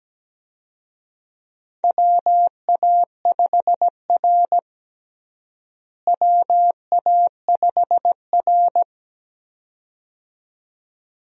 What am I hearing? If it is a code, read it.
Morse code, heard as a keyed tone, about 17 wpm, 700 Hz. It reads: WA5R WA5R
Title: USA Call Signs - 4 Characters - Repeated 2x - 17wpm